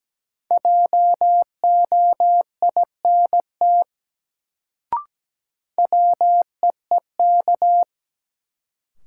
Morse code, heard as a keyed tone, about 17 wpm, 700 Hz. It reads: JOINT WEEK
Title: Sets of 5 Words Encore - New Word Drill - 17wpm